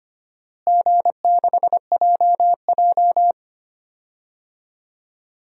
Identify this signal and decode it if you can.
Morse code, heard as a keyed tone, about 25 wpm, 700 Hz. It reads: G6JJ
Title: DXCC-100 Call Signs - 4 Characters - Repeated 2x - 25wpm